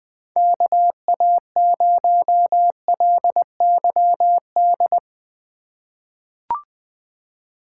Morse code, heard as a keyed tone, about 20 wpm, 700 Hz. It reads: KA0LYD
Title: USA Call Signs - 6 Characters - 2x Character Spacing - 20wpm